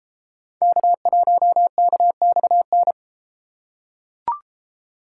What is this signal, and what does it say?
Morse code, heard as a keyed tone, about 29 wpm, 700 Hz. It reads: K1KXN